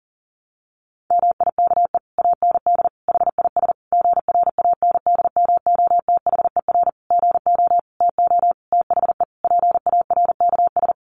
Morse code, heard as a keyed tone, about 37 wpm, 700 Hz. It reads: MIKE AND HIS GRANDMOTHER GO TO THE PARKS
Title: Additional Easy Phrases 40wpm